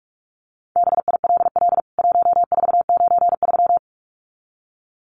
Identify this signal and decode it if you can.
Morse code, heard as a keyed tone, about 44 wpm, 700 Hz. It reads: BILL 1493
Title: CWT Exchange 45wpm